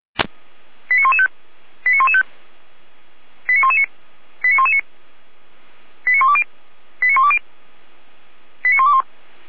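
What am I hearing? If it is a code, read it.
Five-tone selective call (ZVEI), heard as quick runs of five tones, tones of about 70 ms. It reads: 89115 89115 89119 89119 89120 89120 89121
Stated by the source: ZVEI five-tone selective call 89080 received with an SDR.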